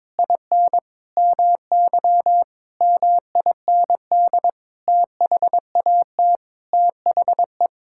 Morse code, heard as a keyed tone, about 22 wpm, 700 Hz. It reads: IN MY MIND THAT THE
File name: Sets of 9 Words - 22wpm